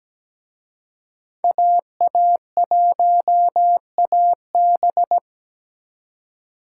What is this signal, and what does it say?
Morse code, heard as a keyed tone, about 20 wpm, 700 Hz. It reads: AA1AB
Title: Speed Racing - USA 2x2 Call Signs 17wpm